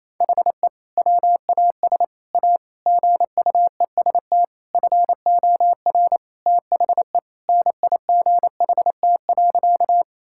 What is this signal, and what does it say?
Morse code, heard as a keyed tone, about 28 wpm, 700 Hz. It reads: HE WAS A GUEST FOR THE NIGHT.
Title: Sentences from Top 1600 Words 28wpm